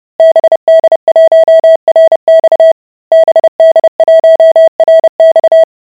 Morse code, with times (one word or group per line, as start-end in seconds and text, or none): 0.2-2.7 BD1RX
3.1-5.6 BD1RX